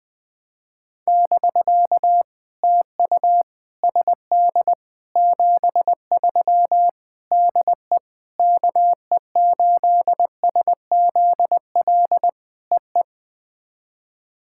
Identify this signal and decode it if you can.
Morse code, heard as a keyed tone, about 20 wpm, 700 Hz. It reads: <BK> TU SD 73 DE KE8SZL EE
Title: POTA QSO Practice - 20wpm